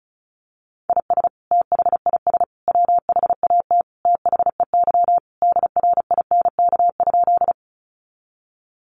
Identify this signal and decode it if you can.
Morse code, heard as a keyed tone, about 34 wpm, 700 Hz. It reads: IS THIS WHAT THEY DRINK?
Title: Sentences from Top 1500 Words 35wpm